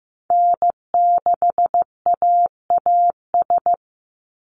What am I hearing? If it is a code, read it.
Morse code, heard as a keyed tone, about 16 wpm, 700 Hz. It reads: N6AAS
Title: USA Call Signs 15wpm